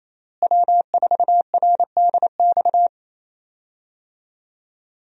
Morse code, heard as a keyed tone, about 27 wpm, 700 Hz. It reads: W4RDX
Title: USA Call Signs - 5 Characters - Repeated 2x - 28wpm